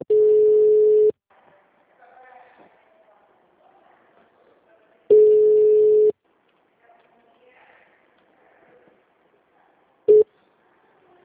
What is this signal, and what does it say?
Ringback tone (Europe, 425 Hz).
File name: out- -2015- - - .19403